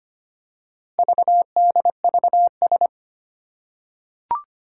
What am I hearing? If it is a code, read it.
Morse code, heard as a keyed tone, about 25 wpm, 700 Hz. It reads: VDVS
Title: Four Challenging Letter-Number Combinations 25wpm